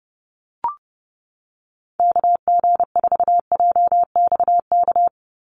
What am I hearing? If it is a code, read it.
Morse code, heard as a keyed tone, about 30 wpm, 700 Hz. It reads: KG4JXK
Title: USA Call Signs 30wpm